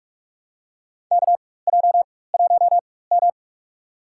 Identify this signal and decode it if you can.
Morse code, heard as a keyed tone, about 44 wpm, spaced as for 21 wpm, 700 Hz. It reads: KJ1M
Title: USA Call Signs - 4 Characters - 4x Character Spacing - 45wpm